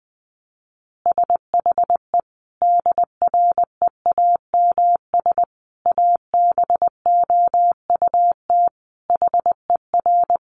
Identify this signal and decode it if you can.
Morse code, heard as a keyed tone, about 19 wpm, 700 Hz. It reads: SHE DREAMS ABOUT HER
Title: Easy Phrases 20wpm